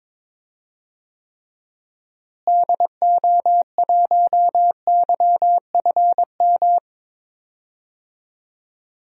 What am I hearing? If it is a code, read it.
Morse code, heard as a keyed tone, about 22 wpm, 700 Hz. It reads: DO1YFM
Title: DXCC-100 Call Signs - 6 Characters - Repeated 2x - 22wpm